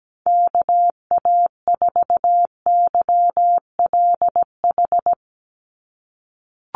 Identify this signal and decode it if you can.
Morse code, heard as a keyed tone, about 17 wpm, 700 Hz. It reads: KA4YLH
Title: USA Call Signs 17wpm